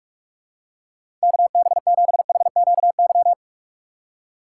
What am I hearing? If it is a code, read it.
Morse code, heard as a keyed tone, about 46 wpm, 700 Hz. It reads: KB7HQY
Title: USA Call Signs - 6x Character Spacing - 45wpm